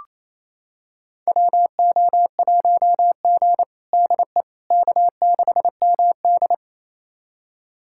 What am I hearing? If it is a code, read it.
Morse code, heard as a keyed tone, about 28 wpm, 700 Hz. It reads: WO1G DE K6MD
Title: USA Callsign DE Callsign - 4 Characters - 28wpm